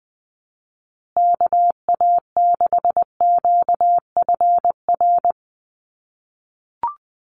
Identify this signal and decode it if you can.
Morse code, heard as a keyed tone, about 20 wpm, 700 Hz. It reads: KA6QFR
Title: USA Call Signs Repeated 3x 20wpm